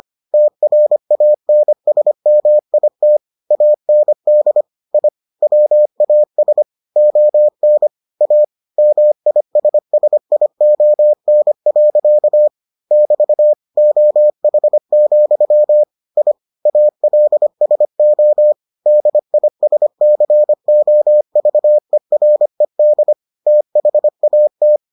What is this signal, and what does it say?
Morse code, heard as a keyed tone, about 25 wpm, 600 Hz. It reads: TRANSMIT AND I WAS ON A MISSION. <BT> OH, I ALSO DISCOVERED THAT